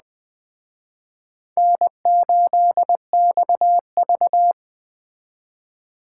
Morse code, heard as a keyed tone, about 20 wpm, 700 Hz. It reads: N8XV